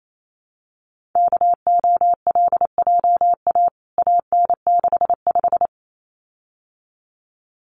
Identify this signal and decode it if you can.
Morse code, heard as a keyed tone, about 28 wpm, 700 Hz. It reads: KOLJA AN65
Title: CWT Exchange 28wpm